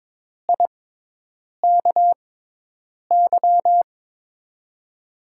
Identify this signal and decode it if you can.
Morse code, heard as a keyed tone, about 22 wpm, spaced as for 8 wpm, 700 Hz. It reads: IKY
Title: USA Call Signs - 6 Characters - 6x Character Spacing - 22wpm